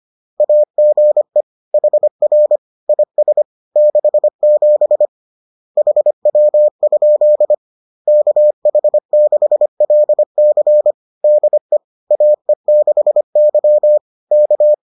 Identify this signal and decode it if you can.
Morse code, heard as a keyed tone, about 25 wpm, 600 Hz. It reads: AGE HR IS 67 HW? KH6LC DE AE6Y K